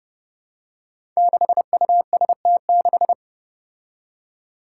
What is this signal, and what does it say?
Morse code, heard as a keyed tone, about 30 wpm, 700 Hz. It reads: 6UST6
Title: Five Challenging Letter-Number Combinations 30wpm